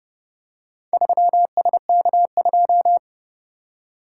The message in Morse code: 3SK2

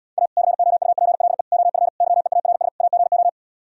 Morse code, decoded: A COUPLE OF YEARS AGO

